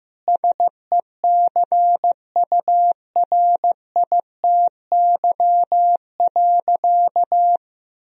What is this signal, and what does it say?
Morse code, heard as a keyed tone, about 16 wpm, 700 Hz. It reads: SECURITY.
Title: Sentences from Top 1000 Words - Review Part 4 - 15wpm